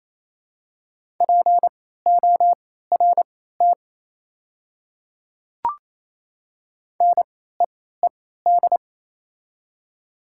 Morse code, heard as a keyed tone, about 28 wpm, spaced as for 16 wpm, 700 Hz. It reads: PORT NEED
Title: 3x Character Spacing - Top 200 Words - 28wpm